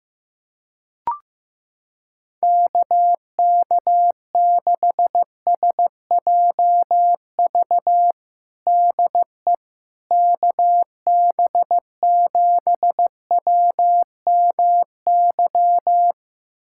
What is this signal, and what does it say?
Morse code, heard as a keyed tone, about 16 wpm, 700 Hz. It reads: KK6SJV DE KB7WMY
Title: USA Callsign DE Callsign - 6 Characters - 15wpm